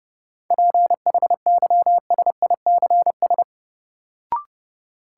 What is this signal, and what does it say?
Morse code, heard as a keyed tone, about 30 wpm, 700 Hz. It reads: PHYSICS
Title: Sentences from Top 1500 Words - New Word Drill Alt - 30wpm